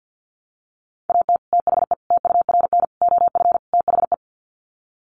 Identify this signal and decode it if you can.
Morse code, heard as a keyed tone, about 50 wpm, 700 Hz. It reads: AT THE TURN OF THE
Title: Sets of 5 Words 50wpm